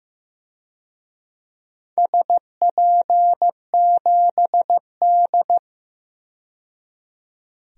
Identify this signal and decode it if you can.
Morse code, heard as a keyed tone, about 15 wpm, 700 Hz. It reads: SP7D
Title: DXCC-100 Call Signs - 4 Characters - 15wpm